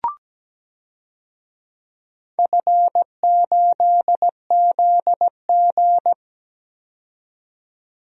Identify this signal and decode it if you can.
Morse code, heard as a keyed tone, about 17 wpm, 700 Hz. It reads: F8ZG
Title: DXCC-100 Call Signs - 4 Characters - 17wpm